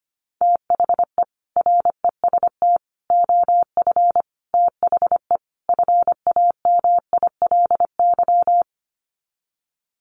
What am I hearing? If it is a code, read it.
Morse code, heard as a keyed tone, about 25 wpm, 700 Hz. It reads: THE REST OF THE FAMILY
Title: Sets of 5 Words Encore2 25wpm